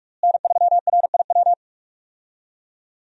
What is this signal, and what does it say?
Morse code, heard as a keyed tone, about 45 wpm, 700 Hz. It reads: N3RIW
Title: USA Call Signs - 5 Characters - Repeated 3x - 45wpm